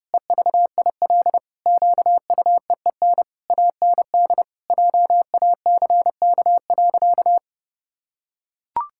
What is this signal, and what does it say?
Morse code, heard as a keyed tone, about 30 wpm, 700 Hz. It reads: EVIL QUEEN AND JACK.
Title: Pangram Challenge v2 - 30wpm